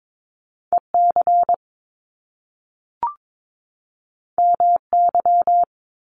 Morse code, heard as a keyed tone, about 22 wpm, 700 Hz. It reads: EC MY
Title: Top Two Letter Combinations in Words 22wpm